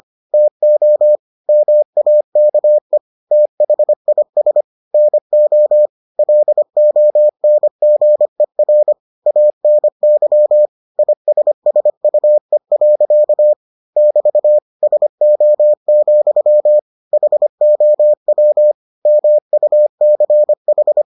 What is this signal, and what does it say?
Morse code, heard as a keyed tone, about 25 wpm, 600 Hz. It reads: TO MAKE THIS NO LONGER ANY ISSUE. <BT> SO, HOW MUCH